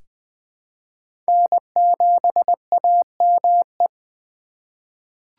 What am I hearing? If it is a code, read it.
Morse code, heard as a keyed tone, about 20 wpm, 700 Hz. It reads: N7AME
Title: K1USN SST QSO Practice - 20wpm